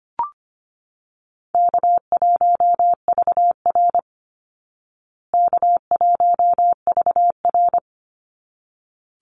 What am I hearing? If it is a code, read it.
Morse code, heard as a keyed tone, about 25 wpm, 700 Hz. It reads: K1VR K1VR
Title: Easy Call Signs Repeated 2x - 25wpm